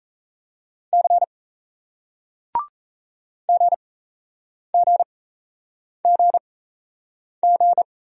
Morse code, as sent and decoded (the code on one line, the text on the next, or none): -.-. / --. --. --. --.
C GGGG